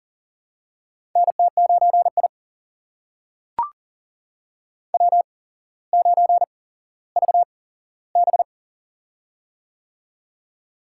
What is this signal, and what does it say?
Morse code, heard as a keyed tone, about 38 wpm, spaced as for 17 wpm, 700 Hz. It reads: NT9I W9VB
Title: USA Call Signs - 4 Characters - 8x Character Spacing - 40wpm